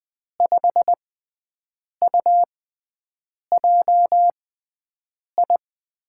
Morse code, heard as a keyed tone, about 20 wpm, spaced as for 7 wpm, 700 Hz. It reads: 5UJI
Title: USA Call Signs - 6x Character Spacing - 20wpm